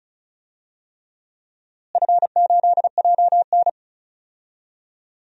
Morse code, heard as a keyed tone, about 36 wpm, 700 Hz. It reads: F8JN
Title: DXCC-100 Call Signs - 4 Characters - Repeated 3x - 35wpm